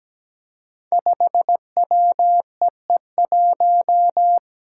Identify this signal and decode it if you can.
Morse code, heard as a keyed tone, about 17 wpm, 700 Hz. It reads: 5WEE1
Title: Five Challenging Letter-Number Combinations 17wpm